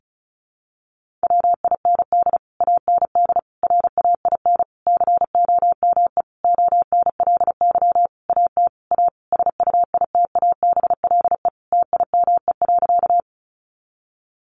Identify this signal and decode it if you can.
Morse code, heard as a keyed tone, about 34 wpm, 700 Hz. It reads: WIND AND RAIN COME ONLY AT A SUITABLE TIME.
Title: Sentences from Top 1400 Words 35wpm